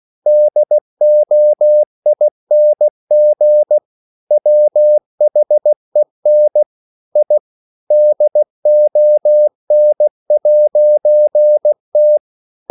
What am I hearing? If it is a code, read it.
Morse code, heard as a keyed tone, about 16 wpm, 600 Hz. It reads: DOING WHEN I DON'T
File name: CW / Practice / EnglishQuotations0416